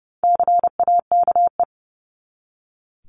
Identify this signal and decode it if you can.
Morse code, heard as a keyed tone, about 30 wpm, 700 Hz. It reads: CAKE
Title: Top 1501-1750 Words 30wpm